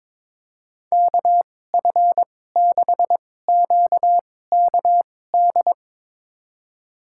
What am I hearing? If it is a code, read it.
Morse code, heard as a keyed tone, about 22 wpm, spaced as for 16 wpm, 700 Hz. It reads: KF6QKD
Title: USA Call Signs - 6 Characters - 2x Character Spacing - 22wpm